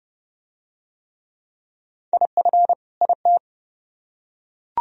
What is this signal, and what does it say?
Morse code, heard as a keyed tone, about 29 wpm, 700 Hz. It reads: IF IT